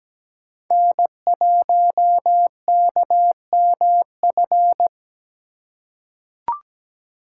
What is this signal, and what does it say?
Morse code, heard as a keyed tone, about 17 wpm, 700 Hz. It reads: N1KMF
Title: USA Call Signs - 3x Character Spacing - 17wpm